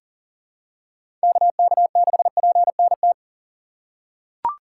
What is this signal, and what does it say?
Morse code, heard as a keyed tone, about 40 wpm, 700 Hz. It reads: KK6PNT
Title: USA Call Signs - 3x Character Spacing - 40wpm